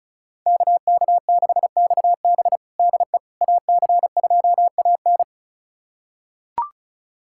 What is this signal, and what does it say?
Morse code, heard as a keyed tone, about 34 wpm, 700 Hz. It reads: KK6XB DE AC2AN
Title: USA Callsign DE Callsign - 5 Characters - 35wpm